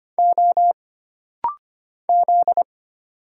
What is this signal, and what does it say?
Morse code, heard as a keyed tone, about 25 wpm, 700 Hz. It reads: O Z
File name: Single Character (Letter:Number:Punctuation:Prosign) - Rapid-Fire - 25wpm